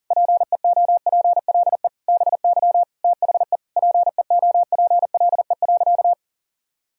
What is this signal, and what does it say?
Morse code, heard as a keyed tone, about 40 wpm, 700 Hz. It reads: PEOPLE BY THE PEOPLE.
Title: Sentences from Top 1000 Words - Review Part 3 - 40wpm